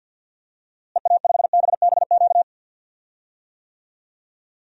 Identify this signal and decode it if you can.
Morse code, heard as a keyed tone, about 49 wpm, 700 Hz. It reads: EA5BBQ
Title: DXCC-100 Call Signs - 6 Characters - Repeated 2x - 50wpm